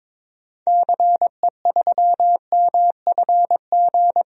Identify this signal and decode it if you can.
Morse code, heard as a keyed tone, about 22 wpm, 700 Hz. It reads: CE3MFG
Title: DXCC-100 Call Signs - 6 Characters - 22wpm